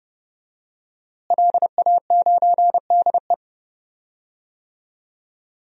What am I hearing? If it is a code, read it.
Morse code, heard as a keyed tone, about 30 wpm, 700 Hz. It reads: LA9DE
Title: DXCC-100 Call Signs - 5 Characters - Repeated 2x - 30wpm